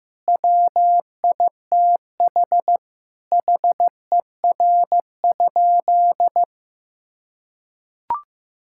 Morse code, heard as a keyed tone, about 16 wpm, 700 Hz. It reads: WITH HER?
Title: Sentences from Top 1000 Words - Review Part 6 - 15wpm